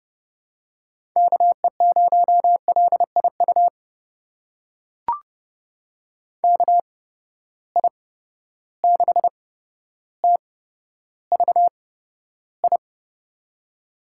Morse code, heard as a keyed tone, about 30 wpm, spaced as for 13 wpm, 700 Hz. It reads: KE0LIU KI6TVI
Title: USA Call Signs - 8x Character Spacing - 30wpm